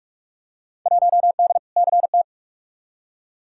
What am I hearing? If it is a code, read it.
Morse code, heard as a keyed tone, about 46 wpm, 700 Hz. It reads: 1D CT
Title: Field Day Exchanges - 45wpm